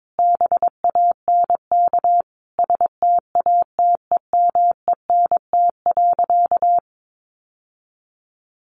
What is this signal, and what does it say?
Morse code, heard as a keyed tone, about 22 wpm, 700 Hz. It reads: BANK STATEMENT.
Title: Sentences from Top 1200 Words 22wpm